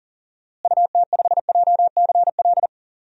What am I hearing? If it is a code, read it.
Morse code, heard as a keyed tone, about 40 wpm, 700 Hz. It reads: UT5JCL
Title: DXCC-100 Call Signs - 6 Characters - Repeated 2x - 40wpm